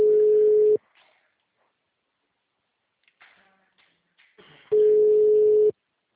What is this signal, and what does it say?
Ringback tone (Europe, 425 Hz).